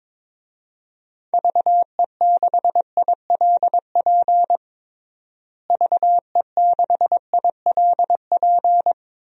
Morse code, heard as a keyed tone, about 21 wpm, 700 Hz. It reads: VE6ILP VE6ILP
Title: DXCC-100 Call Signs - 6 Characters - Repeated 2x - 22wpm